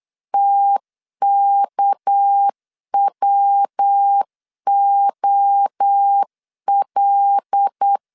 Morse code, heard as a keyed tone, about 8 wpm, 800 Hz. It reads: TKWOL
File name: LEZIONI / 7 GF9Z / 7 G 8wpm 3118